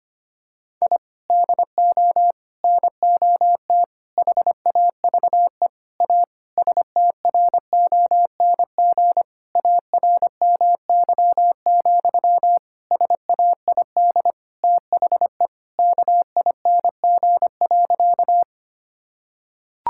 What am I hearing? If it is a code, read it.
Morse code, heard as a keyed tone, about 25 wpm, 700 Hz. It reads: I DO NOT HAVE A STRONG ARMY, SAID THE KING.
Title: Sentences from Top 1000 Words - Review Part 5 - 25wpm